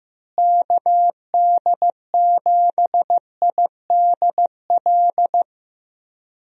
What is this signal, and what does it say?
Morse code, heard as a keyed tone, about 15 wpm, 700 Hz. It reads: KD7IDL
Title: USA Call Signs - 6 Characters - Repeated 2x - 15wpm